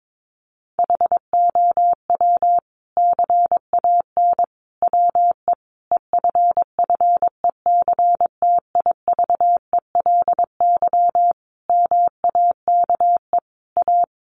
Morse code, heard as a keyed tone, about 22 wpm, 700 Hz. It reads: HOW CAN WE EFFECTIVELY MAKE A
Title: Sentences from Top 1400 Words 22wpm